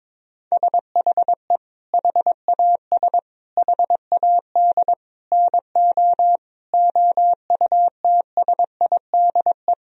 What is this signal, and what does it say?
Morse code, heard as a keyed tone, about 22 wpm, 700 Hz. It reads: SHE HAS HAD NO OUTSIDE
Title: Sentences from Top 1000 Words - Review Part 4 - 22wpm